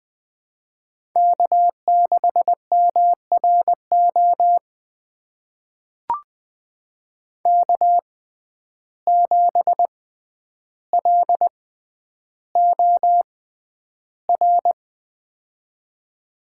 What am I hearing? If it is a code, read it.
Morse code, heard as a keyed tone, about 20 wpm, spaced as for 10 wpm, 700 Hz. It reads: K6MRO K7LOR
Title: USA Call Signs - 5 Characters - 6x Character Spacing - 20wpm